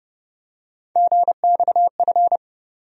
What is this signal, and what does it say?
Morse code, heard as a keyed tone, about 30 wpm, 700 Hz. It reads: GXF
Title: USA Call Sign Postfixes - 3 Letters 30wpm